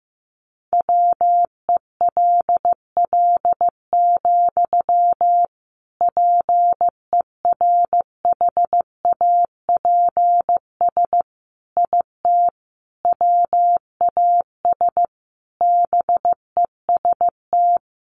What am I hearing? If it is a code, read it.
Morse code, heard as a keyed tone, about 15 wpm, 700 Hz. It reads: WELL, PERHAPS IT WAS BEST
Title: Sentences from Top 500 Words 15wpm